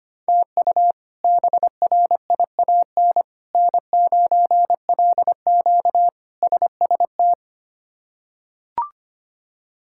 Morse code, heard as a keyed tone, about 25 wpm, 700 Hz. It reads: TU BRIAN N9LQ SST
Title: K1USN SST QSO Practice - 25wpm